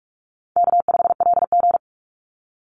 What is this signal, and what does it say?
Morse code, heard as a keyed tone, about 44 wpm, 700 Hz. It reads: K5LG
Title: Easy Call Signs - 45wpm